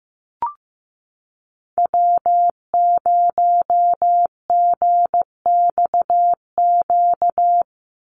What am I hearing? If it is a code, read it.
Morse code, heard as a keyed tone, about 15 wpm, 700 Hz. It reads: W0GXQ
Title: Call Signs Repeated 3x - 15wpm